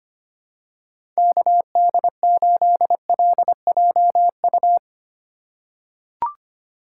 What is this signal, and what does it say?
Morse code, heard as a keyed tone, about 25 wpm, 700 Hz. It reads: KD8LJU
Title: USA Call Signs - 6 Characters - 4x Character Spacing - 25wpm